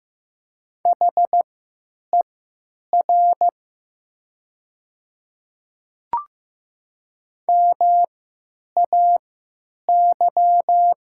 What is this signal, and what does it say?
Morse code, heard as a keyed tone, about 15 wpm, spaced as for 9 wpm, 700 Hz. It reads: HER MAY
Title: 3x Character Spacing - Top 500 Words - 15wpm